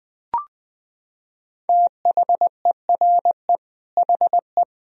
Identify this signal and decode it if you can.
Morse code, heard as a keyed tone, about 21 wpm, 700 Hz. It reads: THERE HE
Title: Sentences from Top 100 Words and Abbreviations 20wpm